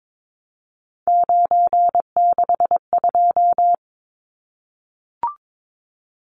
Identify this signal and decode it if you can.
Morse code, heard as a keyed tone, about 22 wpm, 700 Hz. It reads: 962
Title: Numbers - 3-Digit 22wpm